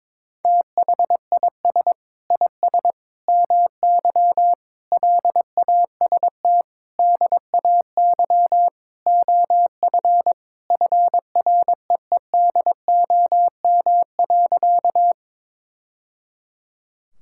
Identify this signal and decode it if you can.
Morse code, heard as a keyed tone, about 22 wpm, 700 Hz. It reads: THIS IS MY LAST DAY OF FREEDOM.
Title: Sentences from Top 1000 Words - Review Part 5 - 22wpm